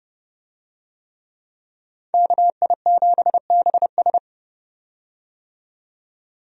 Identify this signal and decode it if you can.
Morse code, heard as a keyed tone, about 30 wpm, 700 Hz. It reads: KI7BS
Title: USA Call Signs - 5 Characters - 30wpm